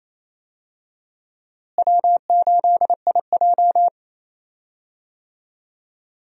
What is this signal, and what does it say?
Morse code, heard as a keyed tone, about 28 wpm, 700 Hz. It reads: W8IJ
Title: USA Call Signs - 4 Characters - Repeated 3x - 28wpm